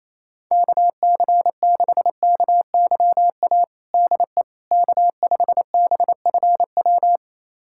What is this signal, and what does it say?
Morse code, heard as a keyed tone, about 28 wpm, 700 Hz. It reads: KC6KYA DE K5BFW
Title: USA Callsign DE Callsign 28wpm